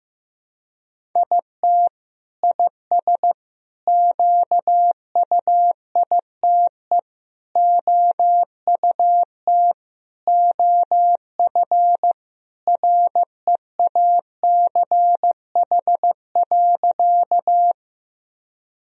Morse code, heard as a keyed tone, about 15 wpm, 700 Hz. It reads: IT IS QUITE OUT OF REACH.
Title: Sentences from Top 1000 Words - Review Part 2 - 15wpm